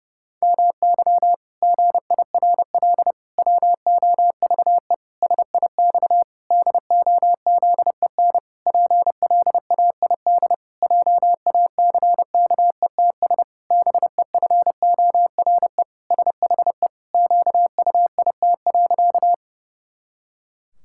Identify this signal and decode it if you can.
Morse code, heard as a keyed tone, about 30 wpm, 700 Hz. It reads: MY GIRL WOVE SIX DOZEN PLAID JACKETS BEFORE SHE QUIT.
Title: Pangram Challenge v2 - 30wpm